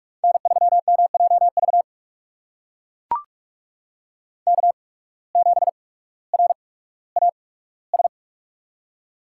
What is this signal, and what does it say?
Morse code, heard as a keyed tone, about 44 wpm, spaced as for 19 wpm, 700 Hz. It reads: N3MJV K7RAS
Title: USA Call Signs - 5 Characters - 8x Character Spacing - 45wpm